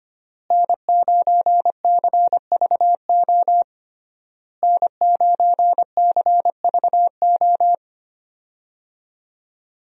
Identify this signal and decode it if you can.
Morse code, heard as a keyed tone, about 25 wpm, 700 Hz. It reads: N9CVO N9CVO
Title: USA Call Signs - 5 Characters - Repeated 3x - 25wpm